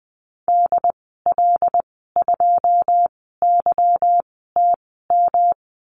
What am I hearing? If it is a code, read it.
Morse code, heard as a keyed tone, about 20 wpm, spaced as for 14 wpm, 700 Hz. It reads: DL2YTM
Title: 154 Lesson 25g - Letter p - Call Signs 20wpm